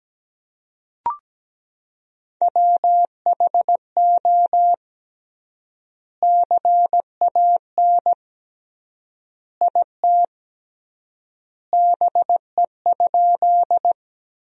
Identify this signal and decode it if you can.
Morse code, heard as a keyed tone, about 17 wpm, 700 Hz. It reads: WHO CAN IT BE?
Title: 3x Word-Spacing - Sentences from Top 100 Words - 17wpm